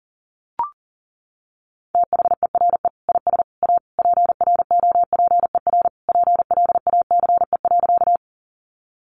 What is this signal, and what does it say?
Morse code, heard as a keyed tone, about 37 wpm, 690 Hz. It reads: THERE IS A PROPER PLACE.
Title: Sentences from Top 800 Words 40wpm